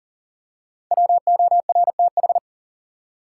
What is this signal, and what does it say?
Morse code, heard as a keyed tone, about 40 wpm, 700 Hz. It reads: WORTH
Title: Five Letter Words - 40wpm